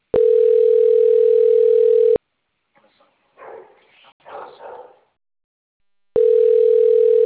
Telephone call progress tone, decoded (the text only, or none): ringback tone